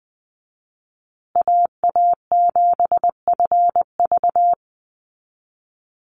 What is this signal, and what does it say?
Morse code, heard as a keyed tone, about 20 wpm, 700 Hz. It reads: AA7FV
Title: Call Signs Repeated 3x - 20wpm